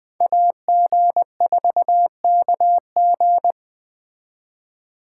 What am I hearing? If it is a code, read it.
Morse code, heard as a keyed tone, about 20 wpm, 700 Hz. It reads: AG4KG
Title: Speed Racing - USA 2x2 Call Signs 17wpm